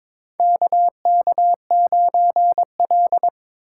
Morse code, heard as a keyed tone, about 22 wpm, 700 Hz. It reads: KK9L
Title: USA Call Signs - 4 Characters - 2x Character Spacing - 22wpm